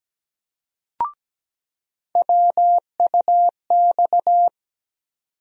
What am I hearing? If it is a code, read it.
Morse code, heard as a keyed tone, about 17 wpm, 700 Hz. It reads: WUX
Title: USA Call Sign Postfixes - 3 Letters 17wpm